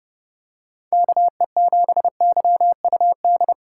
Morse code, heard as a keyed tone, about 29 wpm, 700 Hz. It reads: KE7YUD